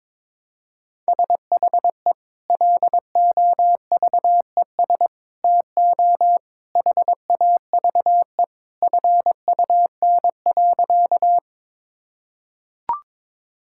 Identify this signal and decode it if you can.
Morse code, heard as a keyed tone, about 22 wpm, 700 Hz. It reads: SHE LOVES TO HAVE FUN.
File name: practice / 22wpm / Sentences with 2-Word Phrases - 22wpm